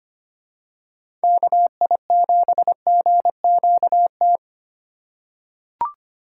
Morse code, heard as a keyed tone, about 25 wpm, 700 Hz. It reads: KI7GQT
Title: USA Call Signs - 2x Character Spacing - 25wpm